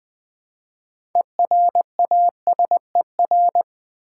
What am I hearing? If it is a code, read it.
Morse code, heard as a keyed tone, about 20 wpm, 700 Hz. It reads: ERASER
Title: Easy Words with Hints - 20wpm